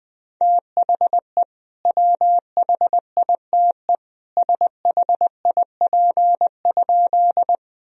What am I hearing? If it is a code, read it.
Morse code, heard as a keyed tone, about 20 wpm, 700 Hz. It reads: THE WHITE SHIP?
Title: Sentences from Top 1000 Words - Review Part 2 - 20wpm